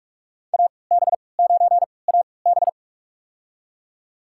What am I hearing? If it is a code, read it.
Morse code, heard as a keyed tone, about 46 wpm, spaced as for 25 wpm, 700 Hz. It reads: AB9AB
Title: USA Call Signs - 5 Characters - 3x Character Spacing - 45wpm